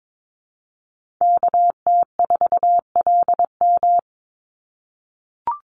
Morse code, heard as a keyed tone, about 22 wpm, 700 Hz. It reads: KT4LM
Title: USA Call Signs 22wpm